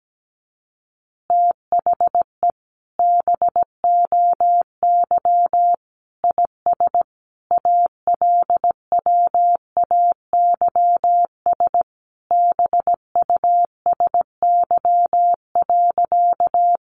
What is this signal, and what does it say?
Morse code, heard as a keyed tone, about 17 wpm, 700 Hz. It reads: THE BOY IS ALWAYS BUSY.
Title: Additional Easy Phrases 17wpm